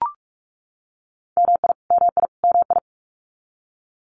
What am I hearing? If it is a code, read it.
Morse code, heard as a keyed tone, about 44 wpm, 700 Hz. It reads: MI MI MI
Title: US States Repeated 3x 45wpm